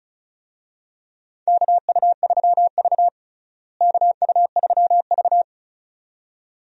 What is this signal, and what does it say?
Morse code, heard as a keyed tone, about 36 wpm, 700 Hz. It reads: KU3V KU3V
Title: USA Call Signs - 4 Characters - Repeated 2x - 35wpm